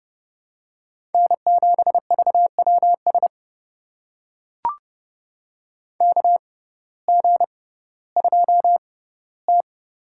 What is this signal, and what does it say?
Morse code, heard as a keyed tone, about 30 wpm, spaced as for 16 wpm, 700 Hz. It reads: N7VWS KG2T
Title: USA Call Signs - 6x Character Spacing - 30wpm